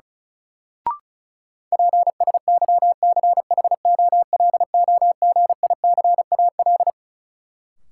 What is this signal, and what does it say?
Morse code, heard as a keyed tone, about 36 wpm, 700 Hz. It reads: PSYCHOLOGICAL
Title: Sentences from Top 1400 Words - New Word Drill - 35wpm